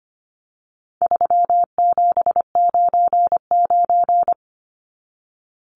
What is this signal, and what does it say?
Morse code, heard as a keyed tone, about 25 wpm, 700 Hz. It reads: 3799
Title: Numbers - 4-Digit 25wpm